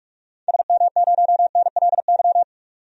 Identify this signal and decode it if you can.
Morse code, heard as a keyed tone, about 46 wpm, 700 Hz. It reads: SM0NLY
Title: DXCC-100 Call Signs - 6 Characters - 45wpm